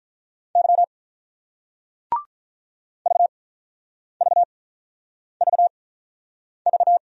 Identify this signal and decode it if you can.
Morse code, heard as a keyed tone, about 43 wpm, spaced as for 9 wpm, 700 Hz. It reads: Y VVVV
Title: Speed-Racing Single Letters 35wpm